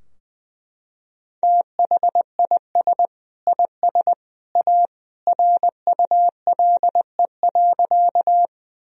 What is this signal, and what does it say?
Morse code, heard as a keyed tone, about 20 wpm, 700 Hz. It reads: THIS IS A RULE.
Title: Sentences from Top 1000 Words - Review Part 2 - 20wpm